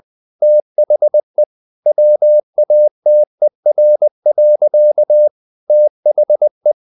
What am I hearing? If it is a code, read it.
Morse code, heard as a keyed tone, about 20 wpm, 600 Hz. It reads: THE WATER. THE
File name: the old man and the sea / the old man and the sea part 1 20wpm